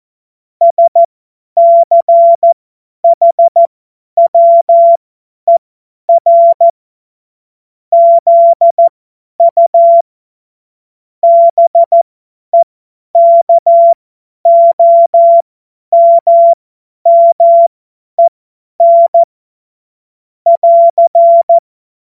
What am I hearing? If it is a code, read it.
Morse code, heard as a keyed tone, about 14 wpm, spaced as for 10 wpm, 680 Hz. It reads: SCHWER ZU BEKOMMEN <AR>
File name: Morsekurs / Witze / Witz 16 Januar 2024 50 F70